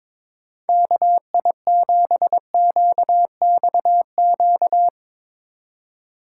Morse code, heard as a keyed tone, about 24 wpm, 700 Hz. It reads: KI7QXQ
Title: Speed Racing - USA 2x3 Call Signs 22wpm